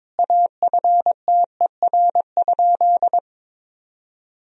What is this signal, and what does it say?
Morse code, heard as a keyed tone, about 22 wpm, 700 Hz. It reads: AFTER?
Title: Sentences from Top 1000 Words - Review Part 1 - 22wpm